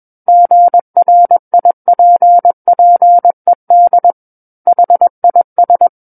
Morse code, heard as a keyed tone, about 21 wpm, 700 Hz. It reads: GRIPPED HIS
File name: Book One / 12-Chapter 12 - 21 WPM